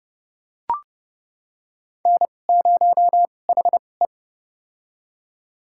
Morse code, heard as a keyed tone, about 31 wpm, spaced as for 22 wpm, 700 Hz. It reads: N0HE
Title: USA Call Signs - 4 Characters - 2x Character Spacing - 30wpm